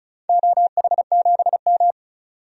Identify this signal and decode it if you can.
Morse code, heard as a keyed tone, about 34 wpm, 700 Hz. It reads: OH7M
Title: DXCC-100 Call Signs - 4 Characters - 35wpm